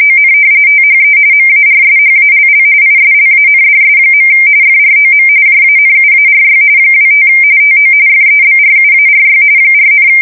RTTY, about 45.45 baud. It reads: ON THE MARSHALL ISLANDS. FROM JANUARY 25 TO FEBRUARY 4,
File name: fsk170 / battleshipnj